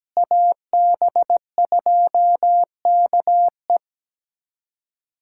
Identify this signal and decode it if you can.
Morse code, heard as a keyed tone, about 17 wpm, 700 Hz. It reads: AB2KE
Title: USA Callsign DE Callsign - 5 Characters - 17wpm